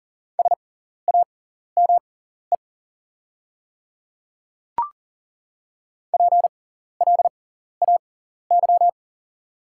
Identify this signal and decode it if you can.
Morse code, heard as a keyed tone, about 37 wpm, spaced as for 14 wpm, 700 Hz. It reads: SAME PLAY
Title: 6x Character Spacing - Top 200 Words - 40wpm